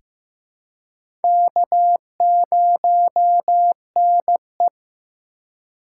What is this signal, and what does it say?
Morse code, heard as a keyed tone, about 15 wpm, 700 Hz. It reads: K0NE
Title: USA Call Signs - 4 Characters - 2x Character Spacing - 15wpm